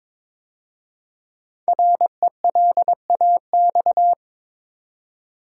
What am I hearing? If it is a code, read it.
Morse code, heard as a keyed tone, about 22 wpm, 700 Hz. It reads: RELAX
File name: 5 Letter Words - First 3 Letters Given - 22wpm